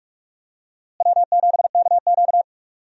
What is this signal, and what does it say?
Morse code, heard as a keyed tone, about 46 wpm, 700 Hz. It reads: W7KQ
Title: USA Call Signs - 4 Characters - Repeated 3x - 45wpm